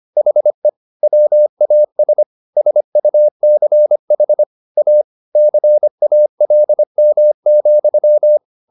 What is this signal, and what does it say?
Morse code, heard as a keyed tone, about 25 wpm, 600 Hz. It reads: HE WAS SUCH A CALM,